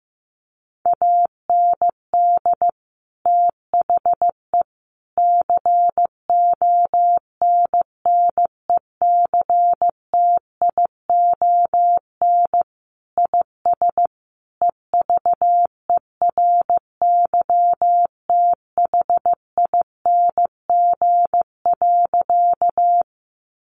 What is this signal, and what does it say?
Morse code, heard as a keyed tone, about 15 wpm, 700 Hz. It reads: AND THE CONNECTION IS EVERYTHING.
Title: Sentences from Top 1300 Words 15wpm